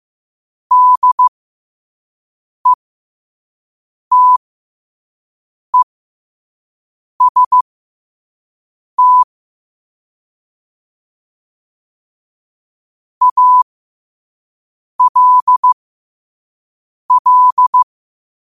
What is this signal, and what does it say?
Morse code, heard as a keyed tone, about 15 wpm, spaced as for 5 wpm, 1000 Hz. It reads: DETEST ALL